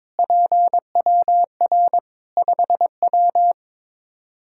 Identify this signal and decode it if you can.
Morse code, heard as a keyed tone, about 22 wpm, 700 Hz. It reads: PWR 5W
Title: Practice QSO - Exchange 2 - Info - 22wpm